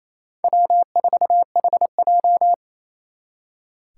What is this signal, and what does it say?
Morse code, heard as a keyed tone, about 36 wpm, 700 Hz. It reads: W4HJ